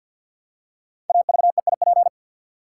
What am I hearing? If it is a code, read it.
Morse code, heard as a keyed tone, about 48 wpm, 700 Hz. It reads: AVEIP
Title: Code Groups - 5 Letters - 50wpm